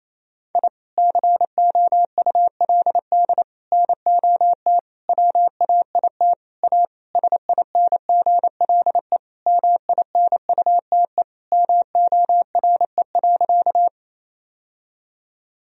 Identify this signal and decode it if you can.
Morse code, heard as a keyed tone, about 28 wpm, 700 Hz. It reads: I COULD NOT WAIT A SINGLE MINUTE MORE.
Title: Sentences from Top 1000 Words - Review Part 4 - 28wpm